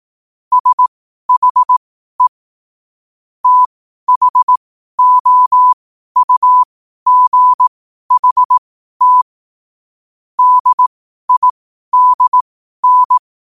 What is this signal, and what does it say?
Morse code, heard as a keyed tone, about 18 wpm, spaced as for 12 wpm, 1000 Hz. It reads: SHE THOUGHT DIDN
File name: qotdfiles / 12wpm